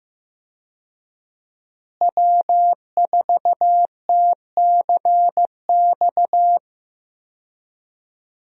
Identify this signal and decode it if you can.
Morse code, heard as a keyed tone, about 15 wpm, 700 Hz. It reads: W4TCX
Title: USA Call Signs - 5 Characters - Repeated 2x - 15wpm